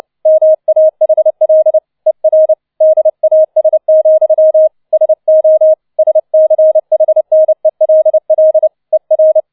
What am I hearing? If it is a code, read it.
Morse code, heard as a keyed tone, about 29 wpm, 630 Hz. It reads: MAHL ER DAS, SO SCHNELL ER